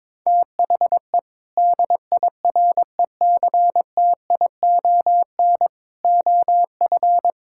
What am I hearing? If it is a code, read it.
Morse code, heard as a keyed tone, about 22 wpm, 700 Hz. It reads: THE DIRECTION OF
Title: Sentences from Top 1000 Words - Review Part 5 - 22wpm